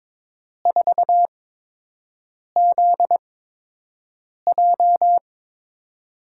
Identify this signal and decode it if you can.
Morse code, heard as a keyed tone, about 21 wpm, spaced as for 6 wpm, 700 Hz. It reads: 4ZJ
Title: USA Call Signs - 4 Characters - 8x Character Spacing - 22wpm